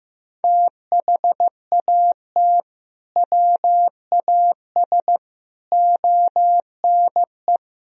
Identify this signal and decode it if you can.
Morse code, heard as a keyed tone, about 15 wpm, 700 Hz. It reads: THAT WAS ONE
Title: Speed-Racing - Sets of 5 Words 15wpm